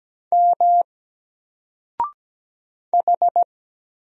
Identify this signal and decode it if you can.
Morse code, heard as a keyed tone, about 17 wpm, 700 Hz. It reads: M H
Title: Single Character (Letter:Number:Punctuation:Prosign) 17wpm